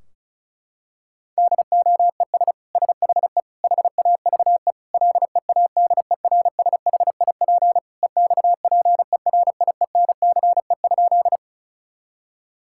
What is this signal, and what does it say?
Morse code, heard as a keyed tone, about 36 wpm, 700 Hz. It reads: DOES SHE HAVE LEADERSHIP EXPERIENCE?
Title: Sentences from Top 1600 Words 35wpm